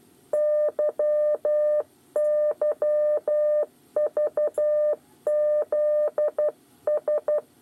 Morse code, heard as a keyed tone, about 12 wpm, 600 Hz. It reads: YYVZS
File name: 8 bis lezione cw 2